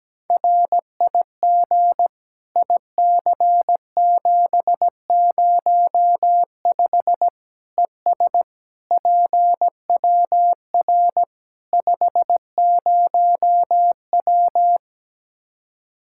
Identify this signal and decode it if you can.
Morse code, heard as a keyed tone, about 17 wpm, 700 Hz. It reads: RIG IC705 ES PWR 50W
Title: K1USN SST QSO Practice 17wpm